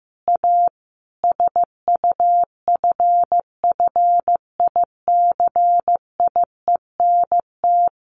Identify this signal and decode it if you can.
Morse code, heard as a keyed tone, about 15 wpm, 700 Hz. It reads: A SUFFICIENT
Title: Sentences from Top 1400 Words 15wpm